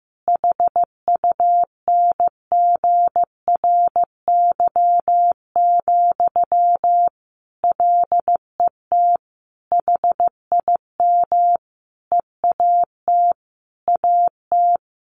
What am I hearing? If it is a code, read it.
Morse code, heard as a keyed tone, about 15 wpm, 700 Hz. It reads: HUNGRY, LET HIM EAT AT
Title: Sentences from Top 1400 Words 15wpm